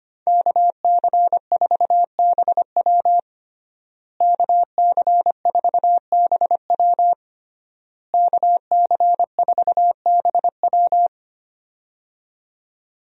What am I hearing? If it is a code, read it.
Morse code, heard as a keyed tone, about 25 wpm, 700 Hz. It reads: KC4BW KC4BW KC4BW
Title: USA Call Signs - 5 Characters - Repeated 3x - 25wpm